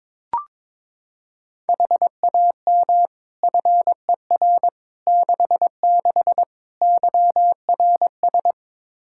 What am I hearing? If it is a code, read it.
Morse code, heard as a keyed tone, about 22 wpm, 700 Hz. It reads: HAM FER 66 YRS
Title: K1USN SST QSO Practice 22wpm